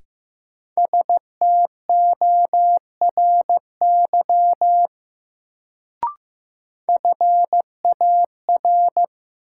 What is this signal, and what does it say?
Morse code, heard as a keyed tone, about 16 wpm, 700 Hz. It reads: STORY FAR
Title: Sets of 3 Words - New Word Drill - 15wpm